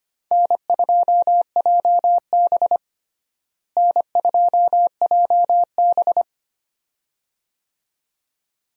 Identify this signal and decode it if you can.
Morse code, heard as a keyed tone, about 25 wpm, 700 Hz. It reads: N2JB N2JB
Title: USA Call Signs - 4 Characters - Repeated 2x - 25wpm